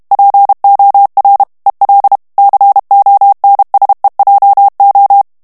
Morse code, heard as a keyed tone, about 32 wpm, 800 Hz. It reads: POR EL CONSEJO
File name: ESPAGNOL TEXTE EN CLAIR 2 VITESSE 30 MOTS MINUTES